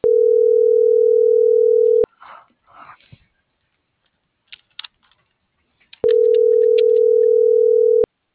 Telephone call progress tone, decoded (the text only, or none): ringback tone